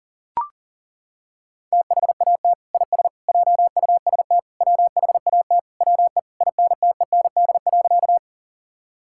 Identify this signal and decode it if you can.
Morse code, heard as a keyed tone, about 40 wpm, 700 Hz. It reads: THAT IS JUST WHAT WE INTEND.
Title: Sentences from Top 1600 Words 40wpm